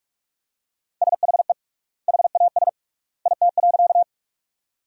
Morse code, heard as a keyed tone, about 43 wpm, 700 Hz. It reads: SHE HAS IT.
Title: 3x Word-Spacing - Sentences from Top 100 Words - 45wpm